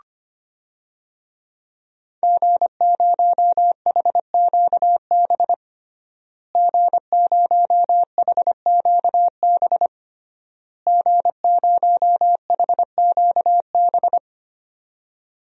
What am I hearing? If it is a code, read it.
Morse code, heard as a keyed tone, about 25 wpm, 700 Hz. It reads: G0HQB G0HQB G0HQB
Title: DXCC-100 Call Signs - 5 Characters - Repeated 3x - 25wpm